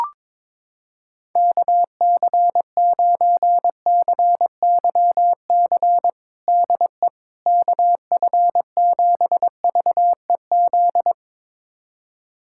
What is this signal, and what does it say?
Morse code, heard as a keyed tone, about 22 wpm, 700 Hz. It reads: KC9CYC DE KF7VEZ
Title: USA Callsign DE Callsign 22wpm